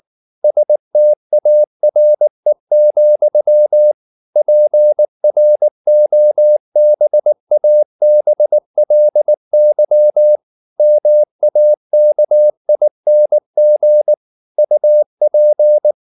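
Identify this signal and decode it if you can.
Morse code, heard as a keyed tone, about 19 wpm, 600 Hz. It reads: STARE, PROBABLY MAKING UP
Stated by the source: Morse code at 19 wpm from CW med Gnister, reading Ken Follett's novel Never in English.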